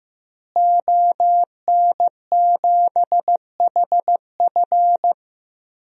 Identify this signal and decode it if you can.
Morse code, heard as a keyed tone, about 15 wpm, 700 Hz. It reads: ON7HF
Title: DXCC-100 Call Signs - 5 Characters - Repeated 3x - 15wpm